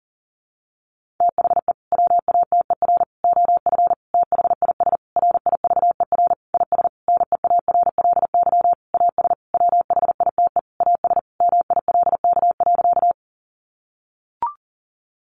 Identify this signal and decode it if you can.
Morse code, heard as a keyed tone, about 40 wpm, 700 Hz. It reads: THE WATER OF THIS RIVER IS NEARLY AS WHITE AS MILK.
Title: Sentences from Top 1500 Words 40wpm